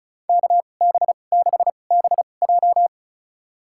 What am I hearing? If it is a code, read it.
Morse code, heard as a keyed tone, about 34 wpm, spaced as for 25 wpm, 700 Hz. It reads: KB6BJ
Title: USA Call Signs - 5 Characters - 2x Character Spacing - 35wpm